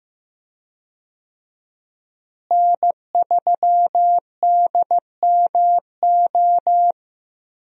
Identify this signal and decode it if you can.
Morse code, heard as a keyed tone, about 15 wpm, 700 Hz. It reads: N3DMO
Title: USA Call Signs - 5 Characters - Repeated 3x - 15wpm